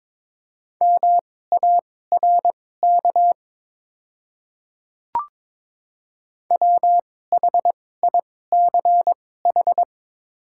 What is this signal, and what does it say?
Morse code, heard as a keyed tone, about 22 wpm, spaced as for 16 wpm, 700 Hz. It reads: MARK WHICH
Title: 2x Character Spacing - Top 300 Words - 22wpm